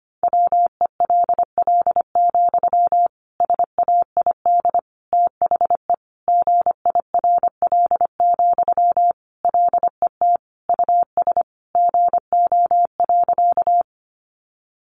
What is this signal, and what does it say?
Morse code, heard as a keyed tone, about 25 wpm, 700 Hz. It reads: WELL, SAID THE GIRL, LET US GO.
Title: Sentences from Top 400 Words 25wpm